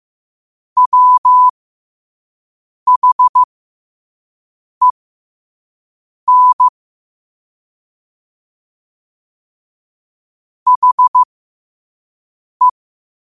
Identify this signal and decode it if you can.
Morse code, heard as a keyed tone, about 15 wpm, spaced as for 5 wpm, 1000 Hz. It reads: WHEN HE
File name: qotdfiles / 5wpm